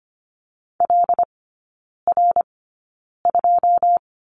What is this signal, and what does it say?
Morse code, heard as a keyed tone, about 25 wpm, spaced as for 9 wpm, 700 Hz. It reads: LR2
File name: Lesson 15e - Number 5 - Three Characters 25wpm